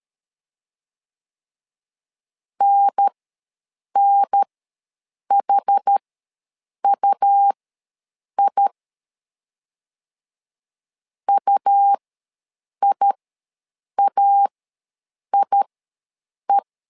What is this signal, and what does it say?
Morse code, heard as a keyed tone, about 13 wpm, spaced as for 7 wpm, 800 Hz. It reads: NNHUI UIAIE
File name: LEZIONI / 2 UHDN / 2 UHDN ws 12wpm 1708